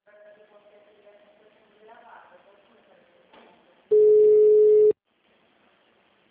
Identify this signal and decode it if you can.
Ringback tone (Europe, 425 Hz), with speech.
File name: out- -2002- - - .4005